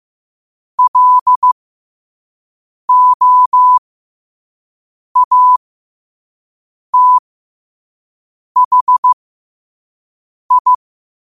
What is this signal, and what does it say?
Morse code, heard as a keyed tone, about 15 wpm, spaced as for 5 wpm, 1000 Hz. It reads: LOATHI
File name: qotdfiles / 5wpm